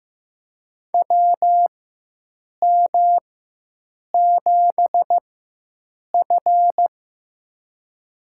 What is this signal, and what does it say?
Morse code, heard as a keyed tone, about 15 wpm, spaced as for 7 wpm, 700 Hz. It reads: WM7F
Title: USA Call Signs - 4x Character Spacing - 15wpm